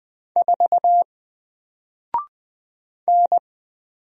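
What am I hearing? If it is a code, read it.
Morse code, heard as a keyed tone, about 20 wpm, 700 Hz. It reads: 4 N
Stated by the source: Single letter and number Morse drill at 20 wpm, from N4LCD.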